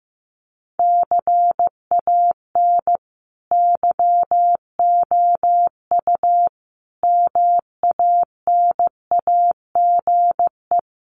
Morse code, heard as a keyed tone, about 15 wpm, 700 Hz. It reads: CAN YOU MANAGE
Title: Sentences from Top 1400 Words 15wpm